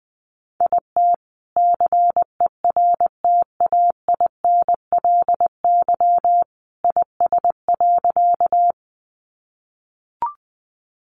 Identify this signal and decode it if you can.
Morse code, heard as a keyed tone, about 20 wpm, 700 Hz. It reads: IT CERTAINLY IS.
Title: Sentences from Top 900 Words 20wpm